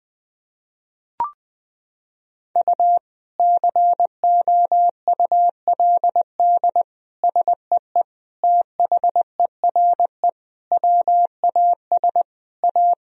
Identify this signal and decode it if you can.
Morse code, heard as a keyed tone, about 20 wpm, 700 Hz. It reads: U COULD SEE THERE WAS A
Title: Sentences from Top 500 Words and Abbreviations 20wpm